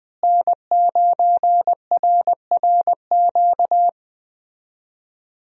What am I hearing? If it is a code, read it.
Morse code, heard as a keyed tone, about 20 wpm, 700 Hz. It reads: N9RRQ
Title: USA Call Signs - 5 Characters - Repeated 2x - 20wpm